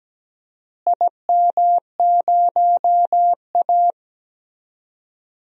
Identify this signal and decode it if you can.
Morse code, heard as a keyed tone, about 17 wpm, 700 Hz. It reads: IM0A
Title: DXCC-100 Call Signs - 4 Characters - Repeated 3x - 17wpm